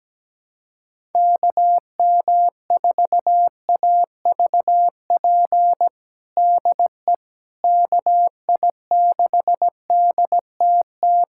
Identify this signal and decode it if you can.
Morse code, heard as a keyed tone, about 17 wpm, 700 Hz. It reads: KM4AVP DE KI6DTT
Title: USA Callsign DE Callsign - 6 Characters - 17wpm